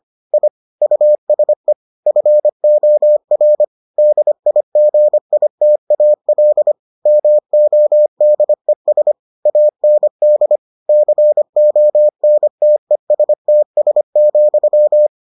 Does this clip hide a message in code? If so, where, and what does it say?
Yes: Morse code from 0.3 to 15.1 s, heard as a keyed tone, about 25 wpm, 600 Hz. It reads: I USE FOR DIGITAL MODES AND CONTESTS,